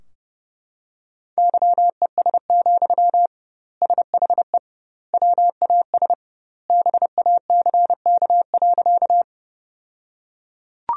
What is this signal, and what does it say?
Morse code, heard as a keyed tone, about 32 wpm, 700 Hz. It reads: YES, SHE WAS BACK.
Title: 2x Word-Spacing - Sentences from Top 500 Words - 30wpm